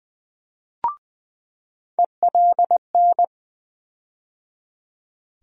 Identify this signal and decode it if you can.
Morse code, heard as a keyed tone, about 20 wpm, 700 Hz. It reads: ELN
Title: USA Call Sign Postfixes - 3 Letters 20wpm